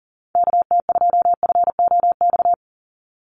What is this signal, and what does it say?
Morse code, heard as a keyed tone, about 40 wpm, 700 Hz. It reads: KT2FOX
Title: USA Call Signs 40wpm